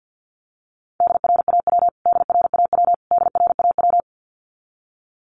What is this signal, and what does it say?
Morse code, heard as a keyed tone, about 50 wpm, 700 Hz. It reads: DRAW DRAW DRAW
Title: Top 500 Words Repeated 50wpm